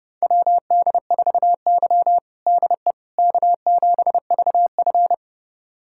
Morse code, heard as a keyed tone, about 32 wpm, 700 Hz. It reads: WD4Y DE K7VF
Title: USA Callsign DE Callsign - 4 Characters - 30wpm